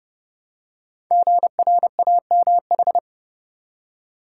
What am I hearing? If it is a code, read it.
Morse code, heard as a keyed tone, about 30 wpm, 700 Hz. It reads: GRAMH
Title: Code Groups - 5 Letters - 30wpm